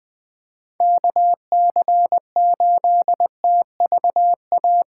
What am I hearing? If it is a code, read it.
Morse code, heard as a keyed tone, about 20 wpm, 700 Hz. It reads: KC8TVA